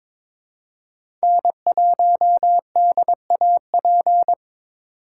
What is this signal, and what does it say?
Morse code, heard as a keyed tone, about 23 wpm, 700 Hz. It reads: N1DAP
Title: USA Call Signs - 5 Characters - 8x Character Spacing - 22wpm